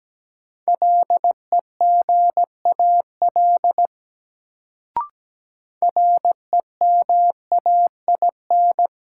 Morse code, heard as a keyed tone, about 17 wpm, 700 Hz. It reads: LEGAL REMAIN
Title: Sentences from Top 1200 Words - New Word Drill - 17wpm